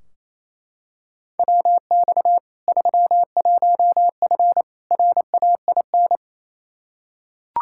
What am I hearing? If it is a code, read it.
Morse code, heard as a keyed tone, about 28 wpm, 700 Hz. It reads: WX 31F RAIN
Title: Practice QSO - Exchange 2 - Info - 28wpm